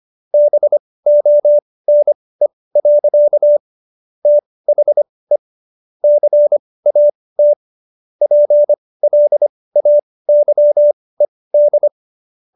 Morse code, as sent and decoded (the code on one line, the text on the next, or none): -... --- -. .-.-.- / - .... / -.-. .- - / .--. .-.. .- -.-- . -..
BONE. THE CAT PLAYED